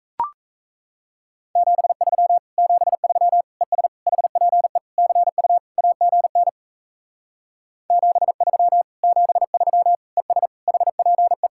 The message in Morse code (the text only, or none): 73 73 ES HPE CU AGN 73 73 ES HPE